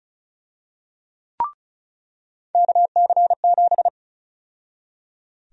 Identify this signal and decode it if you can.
Morse code, heard as a keyed tone, about 34 wpm, 700 Hz. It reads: KC7
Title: USA Call Sign Prefixes - 35wpm